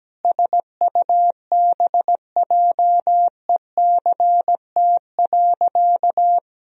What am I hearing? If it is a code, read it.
Morse code, heard as a keyed tone, about 17 wpm, 700 Hz. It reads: SUBJECT.
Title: Sentences from Top 1000 Words - Review Part 4 - 17wpm